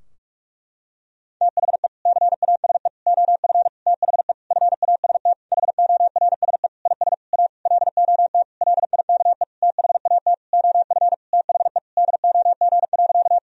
Morse code, heard as a keyed tone, about 46 wpm, 700 Hz. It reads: THE CASE OF THE FAST HORSE IS A LOT LIKE THAT OF THE DOG.
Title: Sentences from Top 1000 Words - Review Part 2 - 45wpm